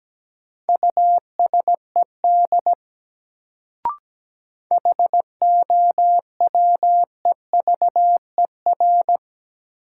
Morse code, heard as a keyed tone, about 17 wpm, 700 Hz. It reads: USED HOWEVER
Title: Sentences from Top 600 Words - New Word Drill - 17wpm